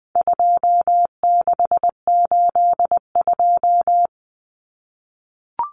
Morse code, heard as a keyed tone, about 20 wpm, 700 Hz. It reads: 2682
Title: Common Cut Numbers - 4-Digit 20wpm